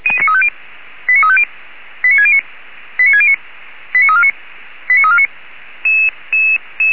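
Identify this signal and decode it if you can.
Five-tone selective call (ZVEI), heard as quick runs of five tones, tones of about 70 ms. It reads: 89350 89609 89609 89349 89349
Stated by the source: ZVEI five-tone selective call 89293 received with an SDR, from the Feuerwehr Orlamünde fire brigade audio server.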